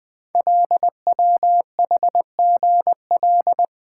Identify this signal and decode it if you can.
Morse code, heard as a keyed tone, about 20 wpm, 700 Hz. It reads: LWHGL
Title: Code Groups - 5 Letter:Numbers - 20wpm